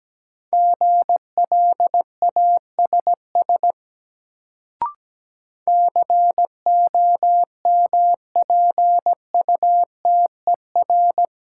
Morse code, heard as a keyed tone, about 17 wpm, 700 Hz. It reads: GLASS COMPUTER
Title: Sentences from Top 1000 Words - Review Part 3 - New Word Drill - 17wpm